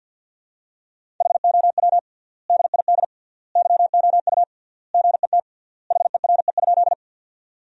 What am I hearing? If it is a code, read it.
Morse code, heard as a keyed tone, about 50 wpm, 700 Hz. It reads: HOW DID YOU GET HERE?
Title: 3x Word-Spacing - Sentences from Top 200 Words - 50wpm